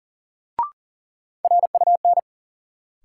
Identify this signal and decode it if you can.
Morse code, heard as a keyed tone, about 40 wpm, 700 Hz. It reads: RUN
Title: Sets of 4 Words - New Word Drill - 40wpm